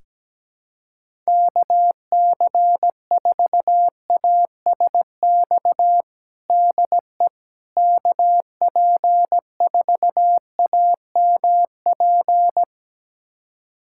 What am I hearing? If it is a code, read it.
Morse code, heard as a keyed tone, about 17 wpm, 700 Hz. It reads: KC4ASX DE KP4AMP
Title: USA Callsign DE Callsign 17wpm